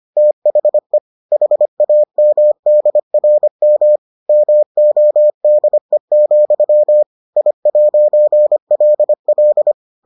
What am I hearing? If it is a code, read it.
Morse code, heard as a keyed tone, about 25 wpm, 600 Hz. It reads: THE HAMDRM MODE, I'LL